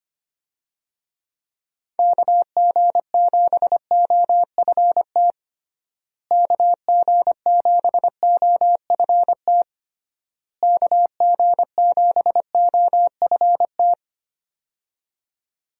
Morse code, heard as a keyed tone, about 25 wpm, 700 Hz. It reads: KG7OFT KG7OFT KG7OFT
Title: USA Call Signs - 6 Characters - Repeated 3x - 25wpm